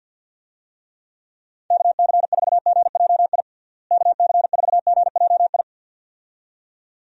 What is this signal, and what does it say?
Morse code, heard as a keyed tone, about 51 wpm, 700 Hz. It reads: KC4GJI KC4GJI
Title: USA Call Signs - 6 Characters - Repeated 2x - 50wpm